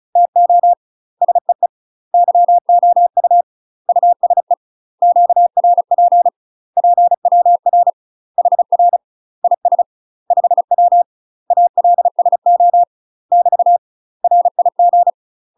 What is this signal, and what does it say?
Morse code, heard as a keyed tone, about 36 wpm, 700 Hz. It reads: TO SEE YOU USE QRP PWR HR IS 5W ALSO <BT> RIG
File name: UK MASTER QSOs TRAINING / 35WPM / UK MASTER QSOs 0003